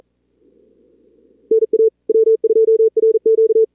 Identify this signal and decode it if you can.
Morse code, heard as a keyed tone, about 37 wpm, 440 Hz. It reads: NA W2RQ
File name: CWSprint / Feb2022 / k5tr-feb-na-sprint-cw-k5ot -2300